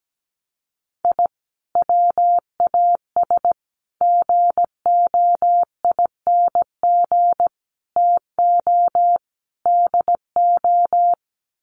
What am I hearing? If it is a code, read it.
Morse code, heard as a keyed tone, about 17 wpm, 700 Hz. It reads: I WAS GOING TO DO
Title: Sets of 5 Words Encore2 17wpm